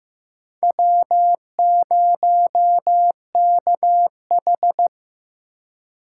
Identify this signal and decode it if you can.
Morse code, heard as a keyed tone, about 15 wpm, 700 Hz. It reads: W0KH
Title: USA Call Signs - 2x Character Spacing - 15wpm